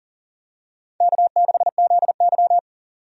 Morse code, heard as a keyed tone, about 40 wpm, 700 Hz. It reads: K6ZY
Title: USA Call Signs - 4 Characters - 40wpm